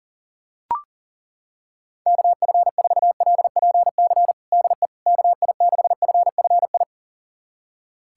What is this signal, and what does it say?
Morse code, heard as a keyed tone, about 40 wpm, 700 Hz. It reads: KF4LPC DE KI6FFI
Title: USA Callsign DE Callsign - 6 Characters - 40wpm